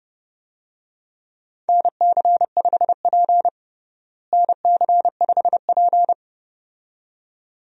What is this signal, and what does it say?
Morse code, heard as a keyed tone, about 31 wpm, 700 Hz. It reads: NC5P NC5P
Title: USA Call Signs - 4 Characters - Repeated 2x - 30wpm